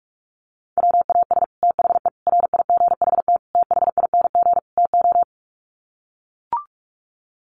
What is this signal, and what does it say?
Morse code, heard as a keyed tone, about 44 wpm, 700 Hz. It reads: WAS THE RIGHT THING TO
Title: Sets of 5 Words Encore2 45wpm